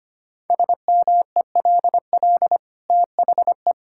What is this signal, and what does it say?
Morse code, heard as a keyed tone, about 25 wpm, 700 Hz. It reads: SMELL THE
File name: Easy Sentences with Hints - 25wpm